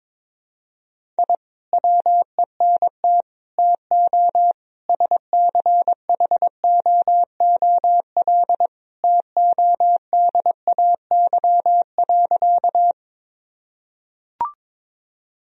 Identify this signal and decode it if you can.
Morse code, heard as a keyed tone, about 22 wpm, 700 Hz. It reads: I WENT TO SCHOOL TODAY.
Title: Sentences from Top 1000 Words - Review Part 1 - 22wpm